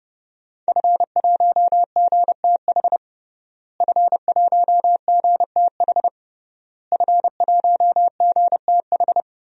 Morse code, heard as a keyed tone, about 30 wpm, 700 Hz. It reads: F1GTH F1GTH F1GTH
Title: DXCC-100 Call Signs - 5 Characters - Repeated 3x - 30wpm